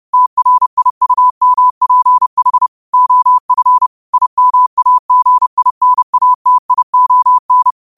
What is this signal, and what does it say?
Morse code, heard as a keyed tone, about 30 wpm, 1000 Hz. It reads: TRIUMPH OF IMAGINATION
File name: qotdfiles / 30wpm